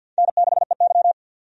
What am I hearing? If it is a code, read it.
Morse code, heard as a keyed tone, about 51 wpm, 700 Hz. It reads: N6EY
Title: USA Call Signs - 4 Characters - 50wpm